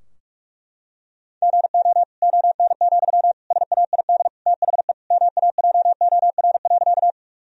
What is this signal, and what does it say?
Morse code, heard as a keyed tone, about 46 wpm, 700 Hz. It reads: GO ON, SAID THE MAJOR.
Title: Sentences from Top 1000 Words - Review Part 4 - 45wpm